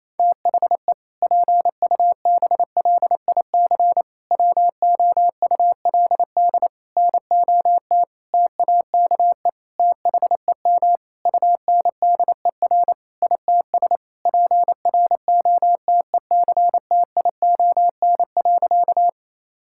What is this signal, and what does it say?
Morse code, heard as a keyed tone, about 28 wpm, 700 Hz. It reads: THE PUBLIC WOULD NOT TAKE THEM UNDER ITS PROTECTION.
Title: Sentences from Top 1600 Words 28wpm